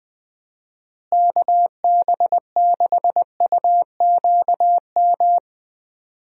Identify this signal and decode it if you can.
Morse code, heard as a keyed tone, about 21 wpm, 700 Hz. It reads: KB6UQM
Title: USA Call Signs - 6 Characters - Repeated 2x - 20wpm